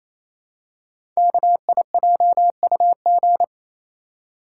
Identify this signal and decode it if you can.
Morse code, heard as a keyed tone, about 27 wpm, 700 Hz. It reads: KIJUG
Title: Code Groups - 5 Letters - 28wpm